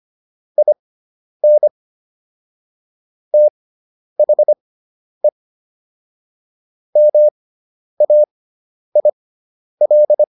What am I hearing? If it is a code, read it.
Morse code, heard as a keyed tone, about 25 wpm, spaced as for 10 wpm, 600 Hz. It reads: IN THE MAIL